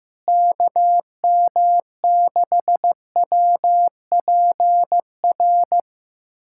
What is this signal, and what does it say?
Morse code, heard as a keyed tone, about 15 wpm, 700 Hz. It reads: KM6WPR
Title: USA Callsign DE Callsign 15wpm